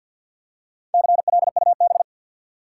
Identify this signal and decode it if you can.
Morse code, heard as a keyed tone, about 51 wpm, 700 Hz. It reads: CLUB
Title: Four Letter Words - 50wpm